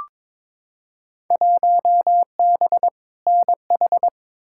Morse code, heard as a keyed tone, about 22 wpm, 700 Hz. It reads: 1B NH
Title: Field Day Exchanges Encore - 22wpm